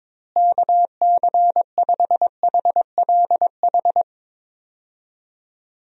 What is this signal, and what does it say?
Morse code, heard as a keyed tone, about 22 wpm, 700 Hz. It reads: KC5HLH
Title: USA Call Signs - 6 Characters - Repeated 2x - 22wpm